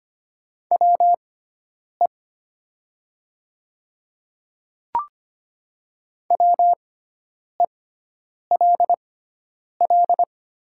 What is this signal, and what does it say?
Morse code, heard as a keyed tone, about 25 wpm, spaced as for 9 wpm, 700 Hz. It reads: WE WELL
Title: 6x Character Spacing - Top 300 Words - 25wpm